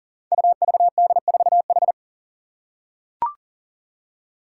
Morse code, heard as a keyed tone, about 40 wpm, 700 Hz. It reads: UVD4H
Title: Five Challenging Letter-Number Combinations 40wpm